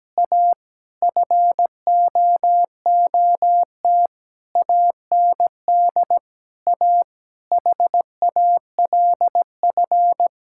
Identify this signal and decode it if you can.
Morse code, heard as a keyed tone, about 17 wpm, 700 Hz. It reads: A FOOT AND A HALF
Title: Sentences from Top 1000 Words - Review Part 6 - 17wpm